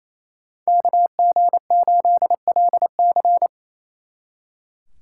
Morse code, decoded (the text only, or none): KG8LC